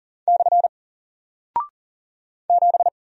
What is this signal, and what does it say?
Morse code, heard as a keyed tone, about 40 wpm, 700 Hz. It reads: / 7
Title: Single Character (Letter:Number:Punctuation:Prosign) Alt 40wpm